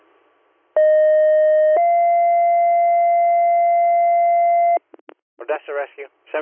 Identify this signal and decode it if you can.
Two-tone page, heard as a one-second tone, then a three-second tone, about 620 then 690 Hz, with speech. It is heard from 0.8 to 4.8 s.